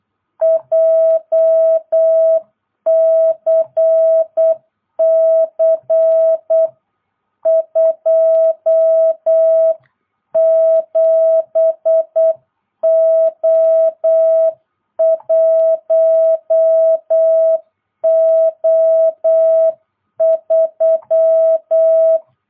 Morse code, heard as a keyed tone, about 8 wpm, 650 Hz. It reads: JCC 27O1O3